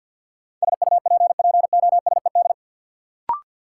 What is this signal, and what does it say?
Morse code, heard as a keyed tone, about 52 wpm, 700 Hz. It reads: SUPPOSED
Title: Sets of 3 Words - New Word Drill - 50wpm